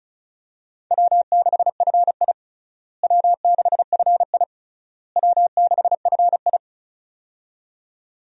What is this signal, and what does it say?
Morse code, heard as a keyed tone, about 34 wpm, 700 Hz. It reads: W6FI W6FI W6FI
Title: USA Call Signs - 4 Characters - Repeated 3x - 35wpm